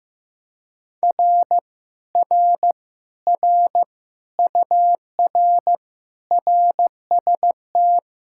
Morse code, heard as a keyed tone, about 15 wpm, 700 Hz. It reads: R R R UR RST
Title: Speed-Racing Common QSO Phrases Encore - 15wpm